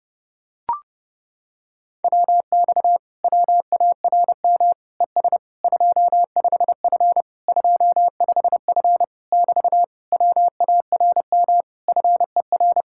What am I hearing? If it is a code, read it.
Morse code, heard as a keyed tone, about 30 wpm, 700 Hz. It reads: WX WARM ES 25F 25F <BT> WARM FER